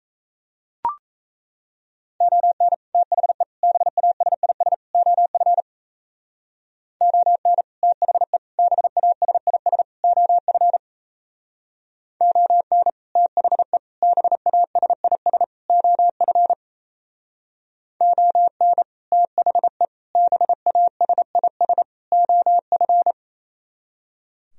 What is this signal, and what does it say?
Morse code, heard as a keyed tone, about 35 wpm, 700 Hz. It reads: ON THE BASIS OF ON THE BASIS OF ON THE BASIS OF ON THE BASIS OF